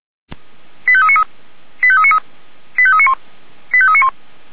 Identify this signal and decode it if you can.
Five-tone selective call (ZVEI), heard as quick runs of five tones, tones of about 70 ms. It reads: 85393 85393 85391 85391